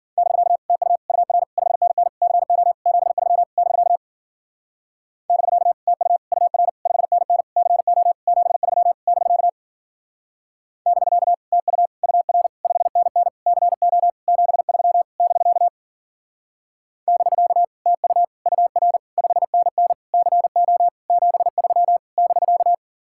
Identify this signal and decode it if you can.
Morse code, heard as a keyed tone, about 47 wpm, 700 Hz. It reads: <BK> TU UR 5NN CO 73 <BK> <BK> TU UR 5NN CO 73 <BK> <BK> TU UR 5NN CO 73 <BK> <BK> TU UR 5NN CO 73 <BK>
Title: Speed-Racing Common QSO Phrases - 40wpm